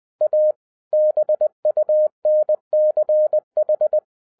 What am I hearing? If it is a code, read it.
Morse code, heard as a keyed tone, about 20 wpm, 610 Hz. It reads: A BUNCH
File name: 2011-10-mp3 / n13-TIPS FROM MY 34 YEARS OF EXPERIENCE 20wpm